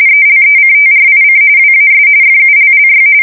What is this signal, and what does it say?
RTTY, about 45.45 baud. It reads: ELECTRIC WAS FULLY
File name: fsk170 / teletype-corp